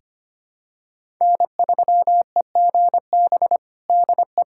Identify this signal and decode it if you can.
Morse code, heard as a keyed tone, about 25 wpm, 700 Hz. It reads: N3EGB DE
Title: USA Callsign DE Callsign - 5 Characters - 25wpm